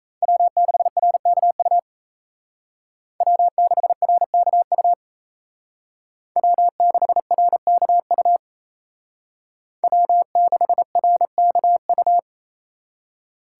Morse code, heard as a keyed tone, about 35 wpm, 700 Hz. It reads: W6RKU W6RKU W6RKU W6RKU